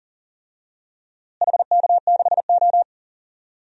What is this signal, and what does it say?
Morse code, heard as a keyed tone, about 40 wpm, 700 Hz. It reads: HK6O
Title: DXCC-100 Call Signs - 4 Characters - Repeated 3x - 40wpm